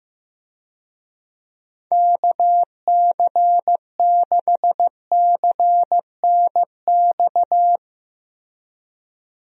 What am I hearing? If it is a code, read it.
Morse code, heard as a keyed tone, about 16 wpm, 700 Hz. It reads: KC6CNX
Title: USA Call Signs - 6 Characters - 15wpm